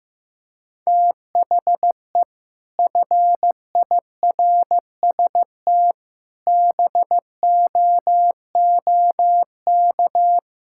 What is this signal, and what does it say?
Morse code, heard as a keyed tone, about 15 wpm, 700 Hz. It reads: THE FIRST BOOK
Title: Sentences from Top 1000 Words - Review Part 5 - 15wpm